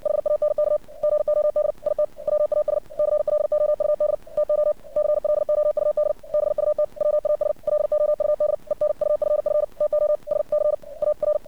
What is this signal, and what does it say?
Morse code, heard as a keyed tone, about 56 wpm, 600 Hz. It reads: HANK GOD IT WAS CLOUD TO CLOUD BUT WAS LOUD ENUFF TO SCAR
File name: archived / CRSnet